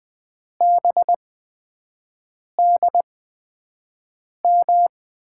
Morse code, heard as a keyed tone, about 20 wpm, spaced as for 5 wpm, 700 Hz. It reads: BDM
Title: USA Call Signs - 6 Characters - 8x Character Spacing - 20wpm